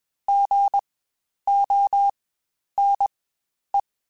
Morse code, heard as a keyed tone, about 20 wpm, spaced as for 10 wpm, 800 Hz. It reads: GONE
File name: Spaced Alphas / Spaced Alphas mp3s / Gs sp @20